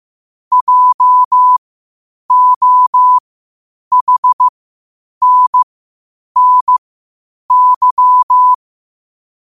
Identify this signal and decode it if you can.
Morse code, heard as a keyed tone, about 15 wpm, spaced as for 8 wpm, 1000 Hz. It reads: JOHNNY
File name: qotdfiles / 8wpm